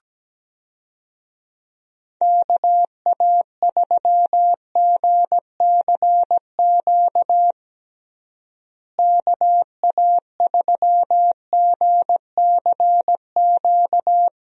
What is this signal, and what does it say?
Morse code, heard as a keyed tone, about 17 wpm, 700 Hz. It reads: KA3GCQ KA3GCQ
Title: USA Call Signs - 6 Characters - Repeated 3x - 17wpm